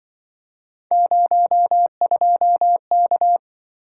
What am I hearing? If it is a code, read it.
Morse code, heard as a keyed tone, about 24 wpm, 700 Hz. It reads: 02K
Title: Speed-Racing - Three Letter-Number Combinations - 20wpm